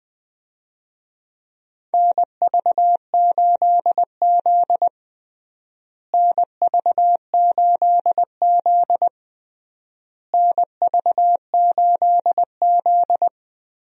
Morse code, heard as a keyed tone, about 20 wpm, 700 Hz. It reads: NV8Z NV8Z NV8Z
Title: USA Call Signs - 4 Characters - Repeated 3x - 20wpm